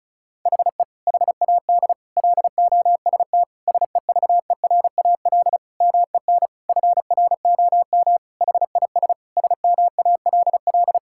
Morse code, heard as a keyed tone, about 36 wpm, 700 Hz. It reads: HE HAD LOST SEVERAL MEN FROM HIS SMALL
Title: Sentences from Top 1600 Words 35wpm